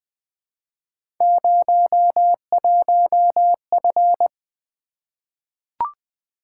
Morse code, heard as a keyed tone, about 20 wpm, 700 Hz. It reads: 01F
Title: Three Letter-Number Combinations - 20wpm